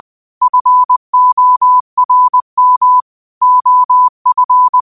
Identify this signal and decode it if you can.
Morse code, heard as a keyed tone, about 20 wpm, 1000 Hz. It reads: FORM OF